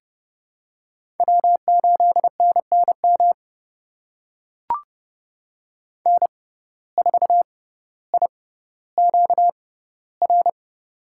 Morse code, heard as a keyed tone, about 31 wpm, spaced as for 16 wpm, 700 Hz. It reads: W8NNM N4IQR
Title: USA Call Signs - 5 Characters - 6x Character Spacing - 30wpm